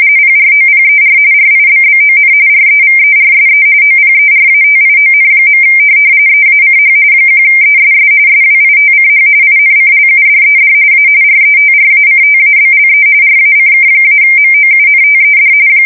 RTTY, about 45.45 baud. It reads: PROPER RIBBON REVERSE OPERATION. EACH 1ESS SWITCH INCLUDED A RETRACTABLE RACK-MOUNTED
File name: fsk170 / teletype-corp